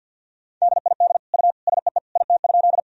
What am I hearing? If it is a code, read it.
Morse code, heard as a keyed tone, about 50 wpm, 700 Hz. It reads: DID U SEE IT?
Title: Sentences from Top 100 Words and Abbreviations 50wpm